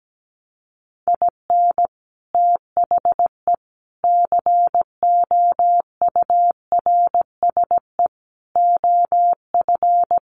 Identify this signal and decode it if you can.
Morse code, heard as a keyed tone, about 17 wpm, 700 Hz. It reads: IN THE COURSE OF
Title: Sets of 4 Words 17wpm